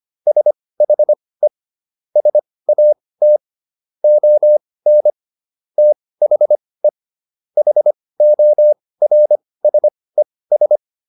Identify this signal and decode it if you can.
Morse code, heard as a keyed tone, about 25 wpm, spaced as for 18 wpm, 600 Hz. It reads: SHE SAT ON THE HORSES